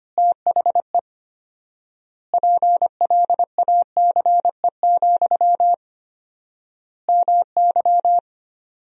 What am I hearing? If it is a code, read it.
Morse code, heard as a keyed tone, about 25 wpm, 700 Hz. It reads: THE PLACE, MY
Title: 4x Word-Spacing - Sentences from Top 400 Words - 25wpm